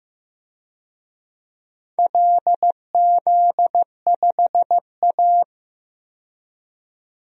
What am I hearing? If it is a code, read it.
Morse code, heard as a keyed tone, about 15 wpm, 700 Hz. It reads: LZ5A
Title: DXCC-100 Call Signs - 4 Characters - 15wpm